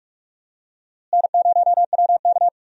Morse code, heard as a keyed tone, about 46 wpm, 700 Hz. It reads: N0WK
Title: USA Call Signs - 4 Characters - Repeated 2x - 45wpm